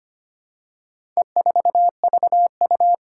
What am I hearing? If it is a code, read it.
Morse code, heard as a keyed tone, about 25 wpm, 700 Hz. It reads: E4VU
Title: Four Challenging Letter-Number Combinations 25wpm